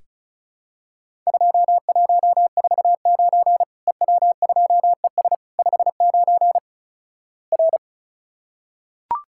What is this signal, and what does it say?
Morse code, heard as a keyed tone, about 36 wpm, 700 Hz. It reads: 2149 EW2ES 59 R
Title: Worked All Europe DX Contest - QTC Exchange - 35wpm